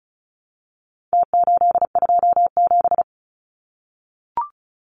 Morse code, heard as a keyed tone, about 34 wpm, 700 Hz. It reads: T827
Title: Numbers - Contest Style - 4-Digit 35wpm